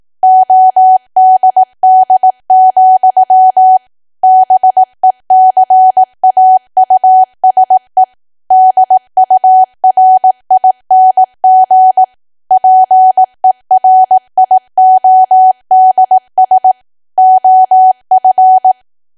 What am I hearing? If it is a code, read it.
Morse code, heard as a keyed tone, about 18 wpm, 750 Hz. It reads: ODD, BECAUSE DURING PERIODS OF
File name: cwa / W1AW-Level2 / 18WPM